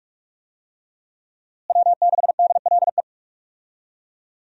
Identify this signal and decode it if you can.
Morse code, heard as a keyed tone, about 46 wpm, 700 Hz. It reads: W6DLE
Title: USA Call Signs - 5 Characters - Repeated 3x - 45wpm